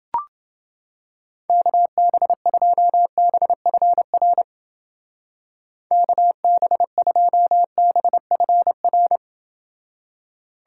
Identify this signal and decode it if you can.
Morse code, heard as a keyed tone, about 28 wpm, 700 Hz. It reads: KB2BFR KB2BFR